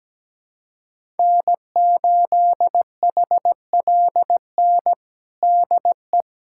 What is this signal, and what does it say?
Morse code, heard as a keyed tone, about 17 wpm, 700 Hz. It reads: N8HLN DE
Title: USA Callsign DE Callsign - 5 Characters - 17wpm